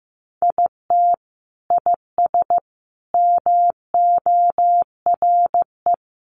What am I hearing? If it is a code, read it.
Morse code, heard as a keyed tone, about 15 wpm, 700 Hz. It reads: IT IS MORE
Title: Sentences from Top 1200 Words 15wpm